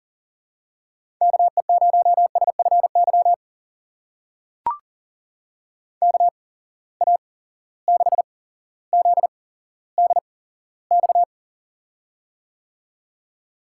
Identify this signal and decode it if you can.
Morse code, heard as a keyed tone, about 42 wpm, spaced as for 17 wpm, 700 Hz. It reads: KE0SFY KA6ZDX
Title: USA Call Signs - 6 Characters - 8x Character Spacing - 40wpm